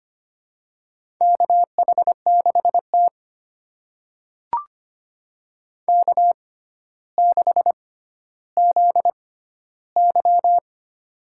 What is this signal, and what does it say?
Morse code, heard as a keyed tone, about 25 wpm, spaced as for 13 wpm, 700 Hz. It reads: KH6T K6ZY
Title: USA Call Signs - 4 Characters - 6x Character Spacing - 25wpm